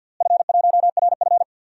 Morse code, heard as a keyed tone, about 50 wpm, 700 Hz. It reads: F1RF
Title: DXCC-100 Call Signs - 4 Characters - Repeated 3x - 50wpm